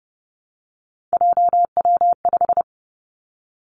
Morse code, heard as a keyed tone, about 30 wpm, 700 Hz. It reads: JW5
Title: Three Challenging Letter-Number Combinations 30wpm